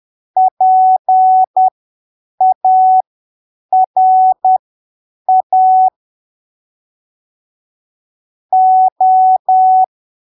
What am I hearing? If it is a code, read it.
Morse code, heard as a keyed tone, about 10 wpm, spaced as for 7 wpm, 750 Hz. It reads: PARA O
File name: arquivos / texto / texto 08 10ppm